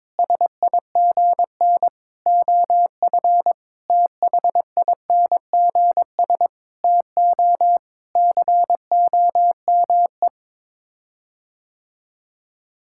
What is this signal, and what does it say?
Morse code, heard as a keyed tone, about 22 wpm, 700 Hz. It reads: SIGN OF THINGS TO COME
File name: practice / 22wpm / Sets of 6 Words - 22wpm